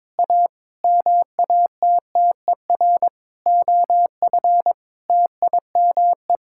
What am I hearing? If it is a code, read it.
Morse code, heard as a keyed tone, about 22 wpm, 700 Hz. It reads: A MATTER OF TIME
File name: practice / 22wpm / Sentences with 7-Word Phrases - 22wpm